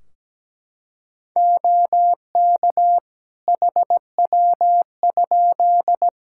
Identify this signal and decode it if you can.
Morse code, heard as a keyed tone, about 17 wpm, 700 Hz. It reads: OK HW?
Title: Practice QSO - Exchange 3 - More Info - 17wpm